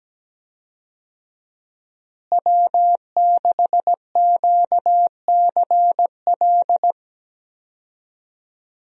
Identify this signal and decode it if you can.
Morse code, heard as a keyed tone, about 17 wpm, 700 Hz. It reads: W6QCL
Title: USA Call Signs - 5 Characters - Repeated 3x - 17wpm